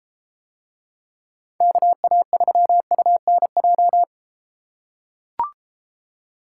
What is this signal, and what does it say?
Morse code, heard as a keyed tone, about 33 wpm, 700 Hz. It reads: KA3UNJ